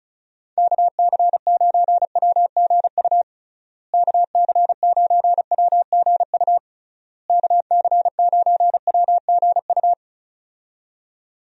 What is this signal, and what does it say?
Morse code, heard as a keyed tone, about 34 wpm, 700 Hz. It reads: KC9WGU KC9WGU KC9WGU
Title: USA Call Signs - 6 Characters - Repeated 3x - 35wpm